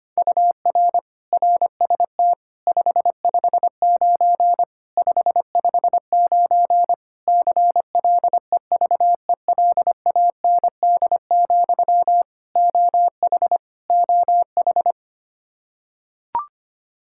Morse code, heard as a keyed tone, about 25 wpm, 700 Hz. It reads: UR RST 559 559 CLEVELAND, OH OH
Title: K1USN SST QSO Practice 25wpm